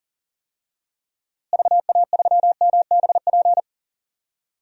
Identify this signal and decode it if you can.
Morse code, heard as a keyed tone, about 40 wpm, 700 Hz. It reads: VA3MBP
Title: DXCC-100 Call Signs - 6 Characters - Repeated 2x - 40wpm